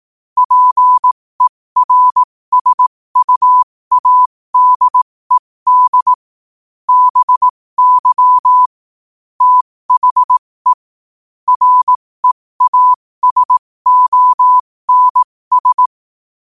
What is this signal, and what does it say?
Morse code, heard as a keyed tone, about 18 wpm, spaced as for 15 wpm, 1000 Hz. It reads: PERSUADED BY THE REASONS
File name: qotdfiles / 15wpm